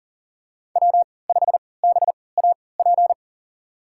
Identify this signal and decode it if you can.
Morse code, heard as a keyed tone, about 40 wpm, spaced as for 23 wpm, 700 Hz. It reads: W5BAP
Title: USA Call Signs - 5 Characters - 3x Character Spacing - 40wpm